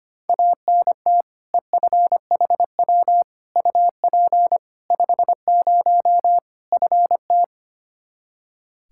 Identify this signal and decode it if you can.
Morse code, heard as a keyed tone, about 25 wpm, 700 Hz. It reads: ANT EFHW UP 50 FT
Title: Practice QSO - Exchange 2 - Info - 25wpm